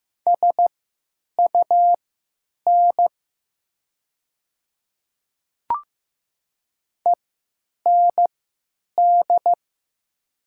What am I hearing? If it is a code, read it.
Morse code, heard as a keyed tone, about 15 wpm, spaced as for 9 wpm, 700 Hz. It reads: SUN END
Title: 3x Character Spacing - Top 300 Words - 15wpm